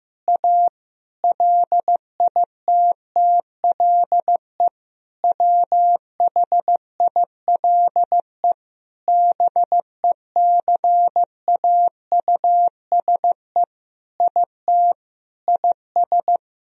Morse code, heard as a keyed tone, about 16 wpm, 700 Hz. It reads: A LITTLE WHILE BECAUSE IT IS
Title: Sentences from Top 1000 Words - Review Part 3 - 15wpm